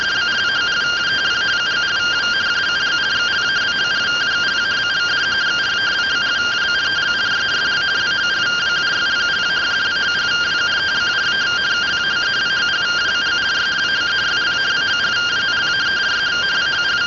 Two-tone teleprinter signal, about 170 Hz shift.